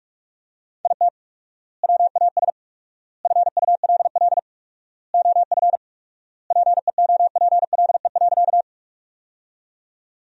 Morse code, heard as a keyed tone, about 46 wpm, 700 Hz. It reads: IT WAS FULL OF PEOPLE.
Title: 4x Word-Spacing - Sentences from Top 500 Words - 45wpm